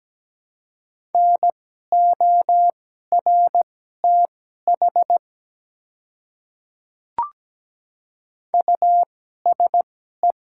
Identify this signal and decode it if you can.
Morse code, heard as a keyed tone, about 17 wpm, spaced as for 12 wpm, 690 Hz. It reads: NORTH USE
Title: 2x Character Spacing - Top 300 Words - 17wpm